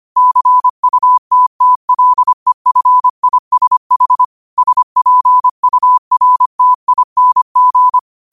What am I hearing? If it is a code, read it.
Morse code, heard as a keyed tone, about 25 wpm, 1000 Hz. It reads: CUTTLEFISH SPURTING